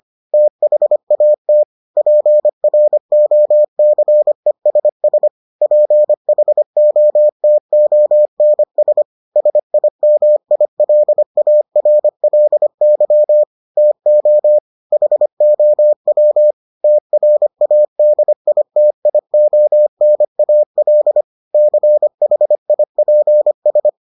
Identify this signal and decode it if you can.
Morse code, heard as a keyed tone, about 25 wpm, 600 Hz. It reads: THAT PROCESS PHOTONS SIMILARLY TO HOW TRADITIONAL CHIPS